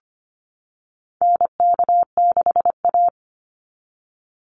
Morse code, heard as a keyed tone, about 25 wpm, 700 Hz. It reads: NK6A
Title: NA SOTA Chaser Callsigns 25wpm